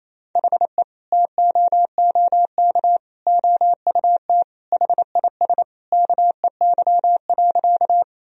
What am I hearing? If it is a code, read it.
Morse code, heard as a keyed tone, about 28 wpm, 700 Hz. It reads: HE TOOK OUT HIS KEY.
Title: Sentences from Top 1000 Words - Review Part 4 - 28wpm